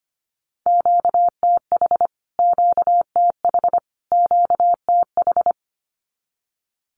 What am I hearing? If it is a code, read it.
Morse code, heard as a keyed tone, about 25 wpm, 700 Hz. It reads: QTH QTH QTH
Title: Q-Codes Repeated 3x 25wpm